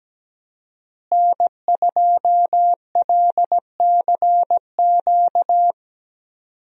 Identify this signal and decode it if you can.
Morse code, heard as a keyed tone, about 17 wpm, 700 Hz. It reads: N2LCQ
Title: USA Call Signs - 5 Characters - Repeated 2x - 17wpm